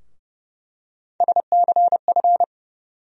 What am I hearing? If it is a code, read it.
Morse code, heard as a keyed tone, about 30 wpm, 700 Hz. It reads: SCF
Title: USA Call Sign Postfixes - 3 Letters 30wpm